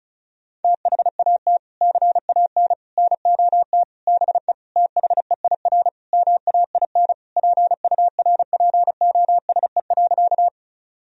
Morse code, heard as a keyed tone, about 34 wpm, 700 Hz. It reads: THAT CAN NOT BE THEIR MAIN PURPOSE.
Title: Sentences from Top 1000 Words - Review Part 2 - 35wpm